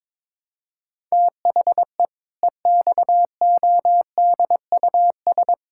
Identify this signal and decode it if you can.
Morse code, heard as a keyed tone, about 22 wpm, 700 Hz. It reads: THE EXODUS
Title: Pangram Challenge v2 - 22wpm